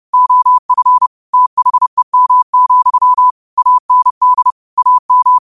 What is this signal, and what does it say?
Morse code, heard as a keyed tone, about 30 wpm, 1000 Hz. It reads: OF THEM, AND AM